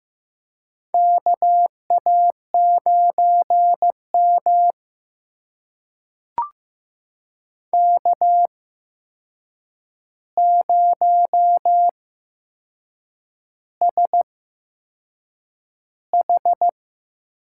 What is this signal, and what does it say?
Morse code, heard as a keyed tone, about 15 wpm, spaced as for 6 wpm, 700 Hz. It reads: KA9M K0SH
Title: USA Call Signs - 4 Characters - 8x Character Spacing - 15wpm